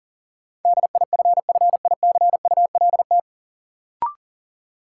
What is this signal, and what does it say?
Morse code, heard as a keyed tone, about 40 wpm, 700 Hz. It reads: DIFFICULT
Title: Sets of 4 Words - New Word Drill - 40wpm